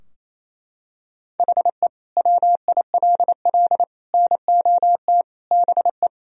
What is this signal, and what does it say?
Morse code, heard as a keyed tone, about 27 wpm, 700 Hz. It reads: HE WILL NOT BE
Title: Sentences from Top 1000 Words - Review Part 4 - 28wpm